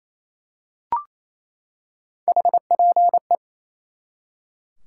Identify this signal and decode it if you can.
Morse code, heard as a keyed tone, about 28 wpm, 700 Hz. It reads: HPE
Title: Top 90 QSO Elements 28wpm